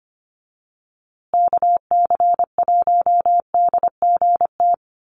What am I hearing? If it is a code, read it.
Morse code, heard as a keyed tone, about 25 wpm, 700 Hz. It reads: KC1DGT
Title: USA Call Signs Repeated 3x 25wpm